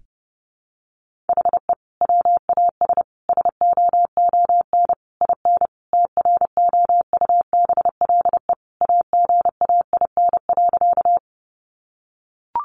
Morse code, heard as a keyed tone, about 30 wpm, 700 Hz. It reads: HE WAS SOON IN TROUBLE AGAIN.
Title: Sentences from Top 1200 Words 30wpm